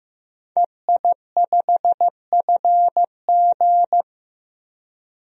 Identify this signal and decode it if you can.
Morse code, heard as a keyed tone, about 15 wpm, 700 Hz. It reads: EI5FG
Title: DXCC-100 Call Signs - 5 Characters - 15wpm